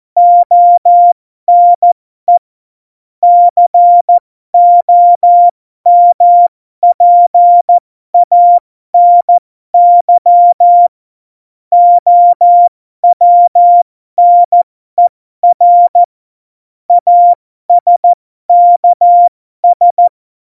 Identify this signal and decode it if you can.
Morse code, heard as a keyed tone, about 14 wpm, spaced as for 12 wpm, 700 Hz. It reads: ONE COMPANY OWNER ASKS
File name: Morsekurs / Witze